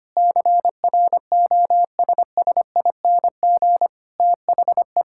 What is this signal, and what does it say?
Morse code, heard as a keyed tone, about 25 wpm, 700 Hz. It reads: CROSSING THE
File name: practice / 25wpm / Difficult Sentences with Hints - 25wpm